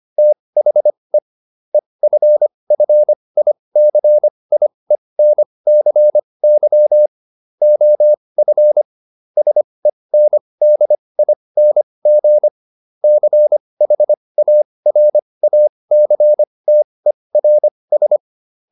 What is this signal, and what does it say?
Morse code, heard as a keyed tone, about 25 wpm, spaced as for 20 wpm, 600 Hz. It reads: THE EFFICIENCY OF SENDING CHARACTERS